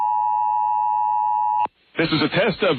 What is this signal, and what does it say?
Emergency Alert System (SAME) tones (the 853 and 960 Hz attention tone), with speech. It is heard from 0.0 to 1.7 s.